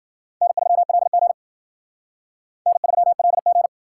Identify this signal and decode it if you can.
Morse code, heard as a keyed tone, about 51 wpm, 700 Hz. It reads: N3LG N3LG